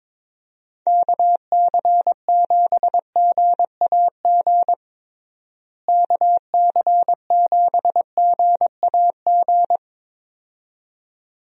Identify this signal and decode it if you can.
Morse code, heard as a keyed tone, about 22 wpm, 700 Hz. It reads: KC7GAG KC7GAG
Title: USA Call Signs - 6 Characters - Repeated 2x - 22wpm